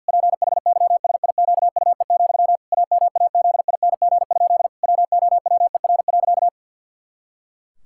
Morse code, heard as a keyed tone, about 50 wpm, 700 Hz. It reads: PHYSIQUE, AMAZING? WOWER.
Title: Pangram Challenge v2 - 50wpm